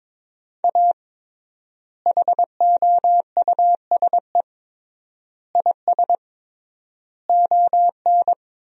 Morse code, heard as a keyed tone, about 22 wpm, 700 Hz. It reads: A HOUSE IS ON
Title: 3x Word-Spacing - Sentences from Top 400 Words - 22wpm